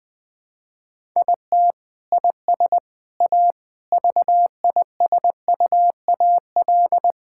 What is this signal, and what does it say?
Morse code, heard as a keyed tone, about 20 wpm, 700 Hz. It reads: IT IS A VISUAL
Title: Sentences from Top 1000 Words - Review Part 6 - 20wpm